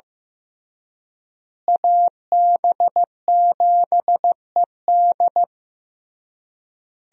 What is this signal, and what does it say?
Morse code, heard as a keyed tone, about 15 wpm, 700 Hz. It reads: AB7ED
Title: USA Call Signs - 5 Characters - Repeated 3x - 15wpm